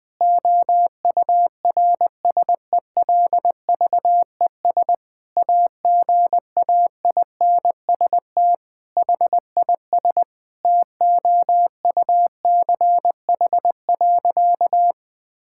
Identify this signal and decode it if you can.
Morse code, heard as a keyed tone, about 20 wpm, 700 Hz. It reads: OURSELVES AGAINST HIS TOUCH.
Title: Sentences from Top 1600 Words 20wpm